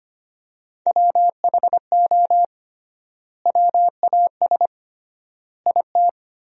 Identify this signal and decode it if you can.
Morse code, heard as a keyed tone, about 25 wpm, 700 Hz. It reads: WHO WAS IT
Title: 3x Word-Spacing - Sentences from Top 400 Words - 25wpm